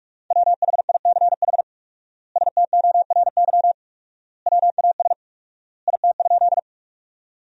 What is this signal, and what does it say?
Morse code, heard as a keyed tone, about 47 wpm, 700 Hz. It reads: WHICH STORY WAS IT?
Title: 4x Word-Spacing - Sentences from Top 300 Words - 45wpm